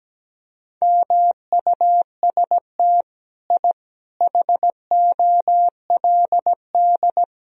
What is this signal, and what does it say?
Morse code, heard as a keyed tone, about 17 wpm, 700 Hz. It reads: MUST I HOLD
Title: Sentences from Top 1000 Words - Review Part 3 - 17wpm